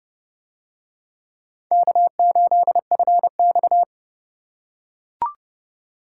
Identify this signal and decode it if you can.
Morse code, heard as a keyed tone, about 29 wpm, 700 Hz. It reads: K8FX